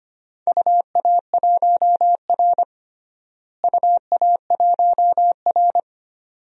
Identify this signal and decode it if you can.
Morse code, heard as a keyed tone, about 25 wpm, 700 Hz. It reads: UA1R UA1R
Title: DXCC-100 Call Signs - 4 Characters - Repeated 2x - 25wpm